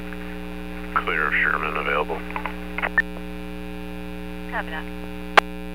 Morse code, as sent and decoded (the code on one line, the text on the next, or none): none